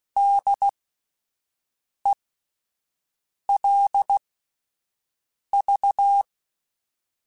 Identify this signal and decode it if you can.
Morse code, heard as a keyed tone, about 16 wpm, spaced as for 5 wpm, 800 Hz. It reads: DELV